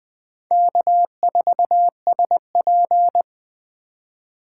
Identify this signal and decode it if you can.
Morse code, heard as a keyed tone, about 20 wpm, 700 Hz. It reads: K4SP
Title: 2023 POTA Hunter Call Signs - 20wpm